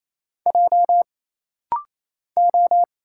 Morse code, heard as a keyed tone, about 27 wpm, 700 Hz. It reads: J O
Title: Single Letter-Number Alt - Rapid-Fire 28wpm